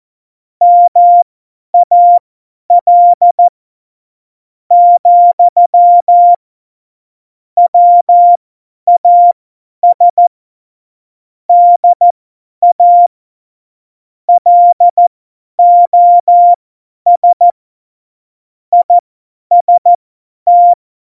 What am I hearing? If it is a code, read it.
Morse code, heard as a keyed tone, about 14 wpm, spaced as for 9 wpm, 700 Hz. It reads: MAL, WAS DA LOS IST